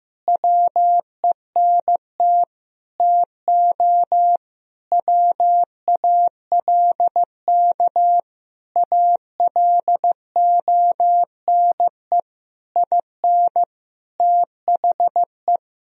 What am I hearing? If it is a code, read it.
Morse code, heard as a keyed tone, about 15 wpm, 700 Hz. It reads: WENT TO WALK ALONE IN THE
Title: Sentences from Top 1000 Words - Review Part 2 - 15wpm